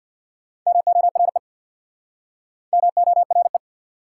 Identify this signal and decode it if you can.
Morse code, heard as a keyed tone, about 69 wpm, 700 Hz. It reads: MORE MORE